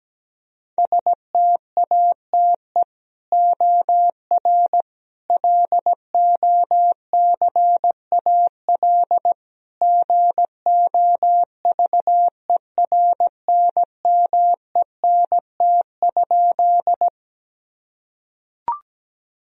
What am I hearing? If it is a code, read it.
Morse code, heard as a keyed tone, about 17 wpm, 700 Hz. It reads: STATE OR LOCAL GOVERNMENT?
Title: Sentences from Top 1000 Words - Review Part 3 - 17wpm